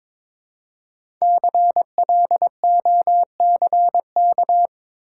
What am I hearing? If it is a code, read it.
Morse code, heard as a keyed tone, about 22 wpm, 700 Hz. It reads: CLOCK
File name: 5 Letter Words - First 3 Letters Given - 22wpm